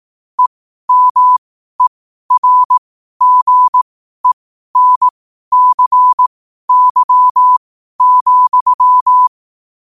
Morse code, heard as a keyed tone, about 18 wpm, spaced as for 12 wpm, 1000 Hz. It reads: EMERGENCY,
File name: qotdfiles / 12wpm